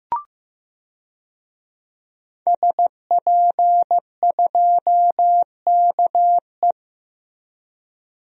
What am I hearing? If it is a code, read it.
Morse code, heard as a keyed tone, about 15 wpm, 700 Hz. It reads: SP2KE
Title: DXCC-100 Call Signs - 5 Characters - Repeated 3x - 15wpm